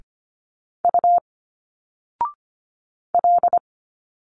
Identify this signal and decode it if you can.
Morse code, heard as a keyed tone, about 25 wpm, 700 Hz. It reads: U L
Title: Single Letters 25wpm